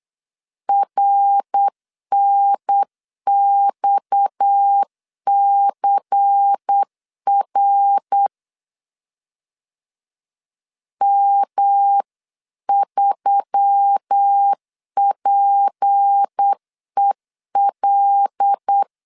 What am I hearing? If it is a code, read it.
Morse code, heard as a keyed tone, about 8 wpm, 800 Hz. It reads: RNXCR M3PEL
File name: CW / DET / 1 6 1500C S B 8wpm 5142